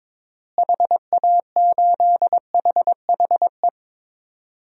Morse code, heard as a keyed tone, about 22 wpm, 700 Hz. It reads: HA8HHE
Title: DXCC-100 Call Signs - 6 Characters - Repeated 2x - 22wpm